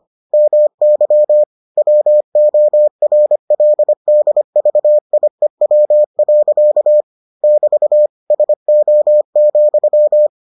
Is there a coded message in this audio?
Yes: Morse code from 0.3 to 10.3 s, heard as a keyed tone, about 25 wpm, 600 Hz. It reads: MY WORLDVIEW. <BT> SO,